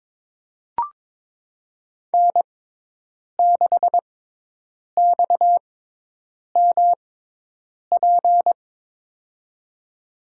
Morse code, heard as a keyed tone, about 22 wpm, spaced as for 8 wpm, 700 Hz. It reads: N6XMP
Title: USA Call Signs - 5 Characters - 6x Character Spacing - 22wpm